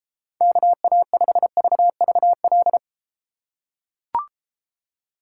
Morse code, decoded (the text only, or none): KA5VVL